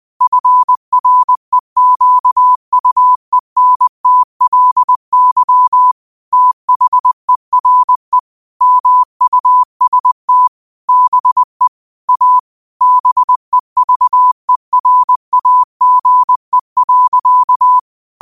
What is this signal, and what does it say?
Morse code, heard as a keyed tone, about 20 wpm, 1000 Hz. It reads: FREQUENTLY THERE MUST BE A BEVERAGE.
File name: qotdfiles / 20wpm